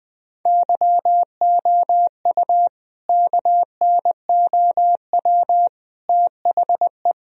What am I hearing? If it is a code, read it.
Morse code, heard as a keyed tone, about 20 wpm, 700 Hz. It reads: YOU KNOW THE
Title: Sentences from Top 1000 Words - Review Part 2 - 20wpm